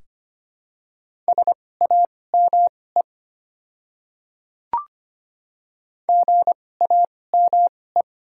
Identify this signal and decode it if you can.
Morse code, heard as a keyed tone, about 25 wpm, spaced as for 18 wpm, 700 Hz. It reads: SAME GAME
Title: 2x Character Spacing - Top 500 Words - 25wpm